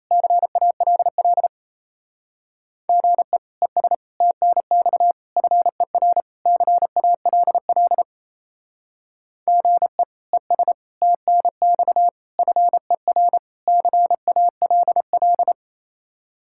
Morse code, heard as a keyed tone, about 35 wpm, 700 Hz. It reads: CALL GE ES TNX FER CALL GE ES TNX FER CALL
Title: Speed-Racing Common QSO Phrases - 28wpm